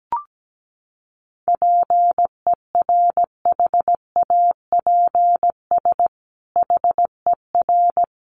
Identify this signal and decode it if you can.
Morse code, heard as a keyed tone, about 17 wpm, 700 Hz. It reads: PERHAPS HER
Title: Sentences from Top 1500 Words 17wpm